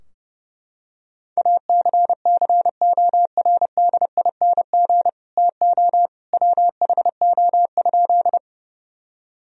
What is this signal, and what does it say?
Morse code, heard as a keyed tone, about 30 wpm, 700 Hz. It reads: ACCORDING TO WHO?
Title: Sentences from Top 1000 Words - Review Part 4 - 30wpm